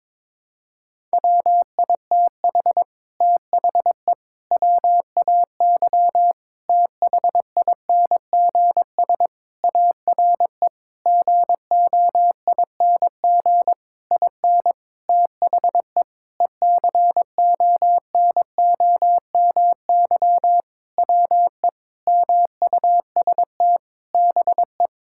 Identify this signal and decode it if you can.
Morse code, heard as a keyed tone, about 23 wpm, 700 Hz. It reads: WITH THE WAY THINGS ARE GOING IN THE ECONOMY WE MUST BE
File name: practice / 22wpm / Sentences with 8-Word Phrases - 22wpm